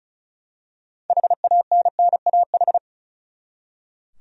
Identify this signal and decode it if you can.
Morse code, heard as a keyed tone, about 36 wpm, 700 Hz. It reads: HANNAH
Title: USA Common Names - 35wpm